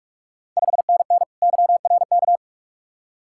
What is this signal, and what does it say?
Morse code, heard as a keyed tone, about 47 wpm, 700 Hz. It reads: 5NN YRK
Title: Ontario QSO Party Exchange 45wpm